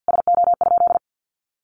Morse code, heard as a keyed tone, about 50 wpm, 700 Hz. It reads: SO?
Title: Sentences from Top 600 Words 50wpm